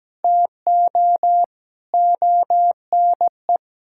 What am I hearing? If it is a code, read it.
Morse code, heard as a keyed tone, about 17 wpm, 700 Hz. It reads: TO ONE
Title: Sentences from Top 1000 Words - Review Part 6 - 17wpm